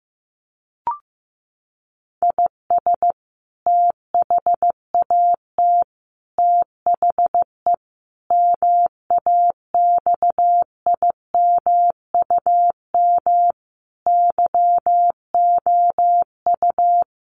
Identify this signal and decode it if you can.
Morse code, heard as a keyed tone, about 14 wpm, 700 Hz. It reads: IS THAT THE MAXIMUM YOU
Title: Sentences from Top 1300 Words 15wpm